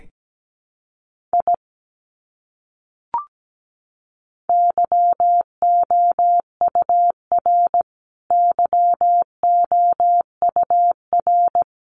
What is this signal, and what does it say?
Morse code, heard as a keyed tone, about 17 wpm, 700 Hz. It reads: I YOUR YOUR
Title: Top 200 Words Repeated 17wpm